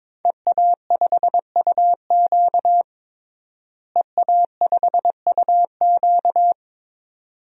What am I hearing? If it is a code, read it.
Morse code, heard as a keyed tone, about 22 wpm, 700 Hz. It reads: EA5UQ EA5UQ
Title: DXCC-100 Call Signs - 5 Characters - Repeated 2x - 22wpm